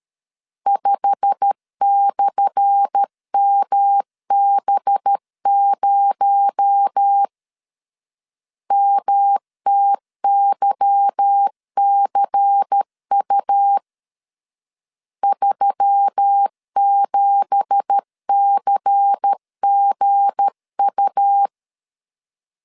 Morse code, heard as a keyed tone, about 13 wpm, 800 Hz. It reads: 5/MB0 MTYCU 37CGU
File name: LEZIONI / 10 SIGNS / 10A SLASH 12wpm 2314